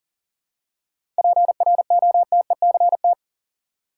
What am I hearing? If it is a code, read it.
Morse code, heard as a keyed tone, about 38 wpm, 700 Hz. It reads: PROTECT
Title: Seven Letter Words - 40wpm